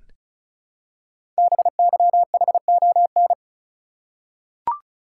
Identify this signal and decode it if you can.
Morse code, heard as a keyed tone, about 36 wpm, 700 Hz. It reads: BYHON
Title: Code Groups - 5 Letters - 35wpm